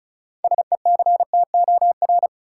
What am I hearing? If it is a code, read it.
Morse code, heard as a keyed tone, about 34 wpm, 700 Hz. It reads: SECTOR
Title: Six Letter Words - 35wpm